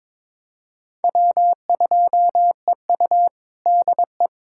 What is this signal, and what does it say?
Morse code, heard as a keyed tone, about 22 wpm, 700 Hz. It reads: W2EU DE
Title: USA Callsign DE Callsign - 4 Characters - 22wpm